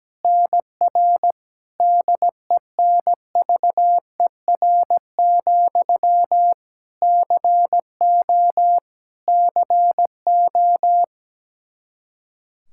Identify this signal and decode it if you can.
Morse code, heard as a keyed tone, about 17 wpm, 700 Hz. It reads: NR DENVER, CO CO
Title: Practice QSO - Exchange 1 - Beginning - 17wpm